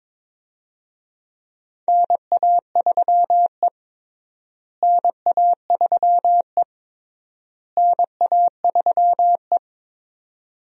Morse code, heard as a keyed tone, about 21 wpm, 700 Hz. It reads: NA3E NA3E NA3E
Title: USA Call Signs - 4 Characters - Repeated 3x - 22wpm